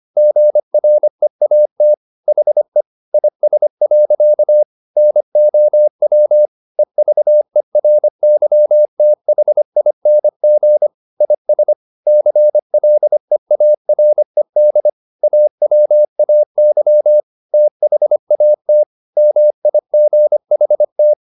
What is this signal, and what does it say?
Morse code, heard as a keyed tone, about 25 wpm, 600 Hz. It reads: GREAT HE IS. NOW EVERYTHING IS CLEARED AWAY THAT MIGHT
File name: the old man and the sea part 1 25wpm